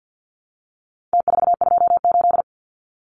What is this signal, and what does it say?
Morse code, heard as a keyed tone, about 48 wpm, 700 Hz. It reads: T428
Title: Numbers - Contest Style - 4-Digit 50wpm